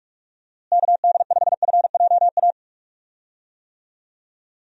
Morse code, heard as a keyed tone, about 41 wpm, 700 Hz. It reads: KD5FJA
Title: USA Call Signs - 6 Characters - Repeated 2x - 45wpm